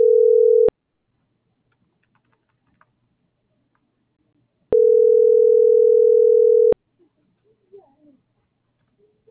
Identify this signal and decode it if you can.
Ringback tone (North America), 440 + 480 Hz.